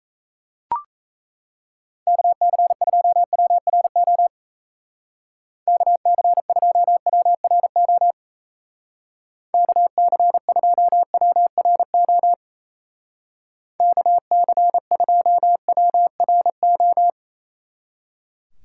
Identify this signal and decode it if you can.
Morse code, heard as a keyed tone, about 34 wpm, 700 Hz. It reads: KC2WRO KC2WRO KC2WRO KC2WRO
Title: Speed Racing - USA 2x3 Call Signs 28wpm